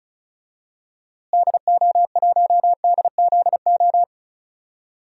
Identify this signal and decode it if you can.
Morse code, heard as a keyed tone, about 34 wpm, 700 Hz. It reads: DO1DZO
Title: DXCC-100 Call Signs - 6 Characters - Repeated 2x - 35wpm